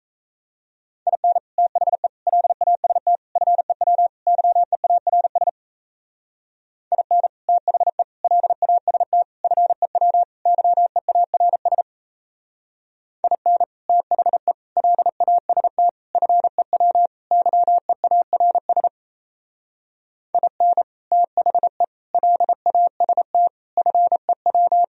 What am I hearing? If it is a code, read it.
Morse code, heard as a keyed tone, about 33 wpm, 700 Hz. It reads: IN THE LAST FEW YEARS IN THE LAST FEW YEARS IN THE LAST FEW YEARS IN THE LAST FEW
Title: Speed-Racing - Sets of 5 Words 28wpm